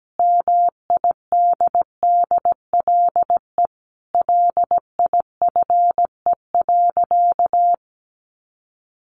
Morse code, heard as a keyed tone, about 17 wpm, 700 Hz. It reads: MIDDLE LIFE.
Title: Sentences from Top 1300 Words 17wpm